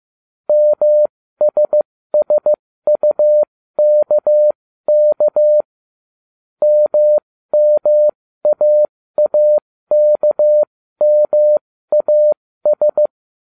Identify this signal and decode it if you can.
Morse code, heard as a keyed tone, about 15 wpm, spaced as for 13 wpm, 600 Hz. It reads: MSSUKK MMAAKMAS